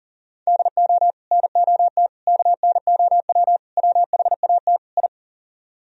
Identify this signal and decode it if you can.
Morse code, heard as a keyed tone, about 51 wpm, 700 Hz. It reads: DO NOT KNOW WHAT I